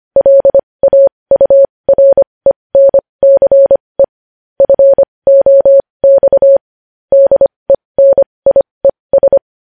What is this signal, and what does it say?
Morse code, heard as a keyed tone, about 25 wpm, spaced as for 20 wpm, 550 Hz. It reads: LAURENCE FOX DENIES
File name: podcast / News headlines-1x-25x20